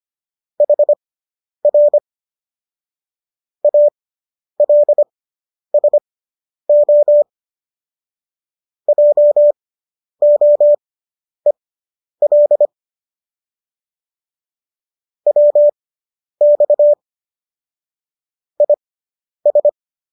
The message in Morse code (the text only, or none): HR ALSO JOEL WX IS